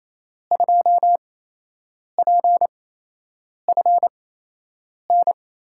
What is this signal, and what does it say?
Morse code, heard as a keyed tone, about 28 wpm, spaced as for 8 wpm, 700 Hz. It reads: 2PFN
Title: USA Call Signs - 8x Character Spacing - 28wpm